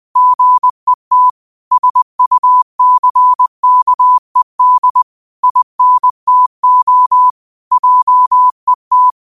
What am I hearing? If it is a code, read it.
Morse code, heard as a keyed tone, about 20 wpm, 1000 Hz. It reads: GET SUCKED INTO JET
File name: qotdfiles / 20wpm